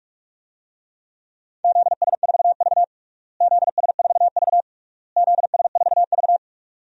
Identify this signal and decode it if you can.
Morse code, heard as a keyed tone, about 46 wpm, 700 Hz. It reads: ZS4V ZS4V ZS4V
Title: DXCC-100 Call Signs - 4 Characters - Repeated 3x - 45wpm